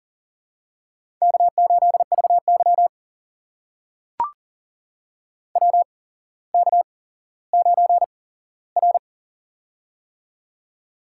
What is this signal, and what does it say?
Morse code, heard as a keyed tone, about 39 wpm, spaced as for 17 wpm, 700 Hz. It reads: K8VY WK9R
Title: USA Call Signs - 4 Characters - 8x Character Spacing - 40wpm